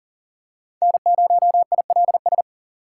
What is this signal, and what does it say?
Morse code, heard as a keyed tone, about 40 wpm, 700 Hz. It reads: N0ILS
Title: USA Call Signs - 6x Character Spacing - 40wpm